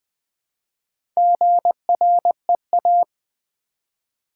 Morse code, heard as a keyed tone, about 20 wpm, 700 Hz. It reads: GREA
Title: Build A Word - 1 Letter At A Time - 5-Letter Words - 20wpm